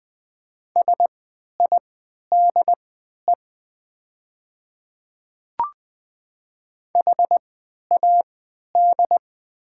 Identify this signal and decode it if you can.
Morse code, heard as a keyed tone, about 20 wpm, spaced as for 11 wpm, 700 Hz. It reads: SIDE HAD
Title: 3x Character Spacing - Top 200 Words - 20wpm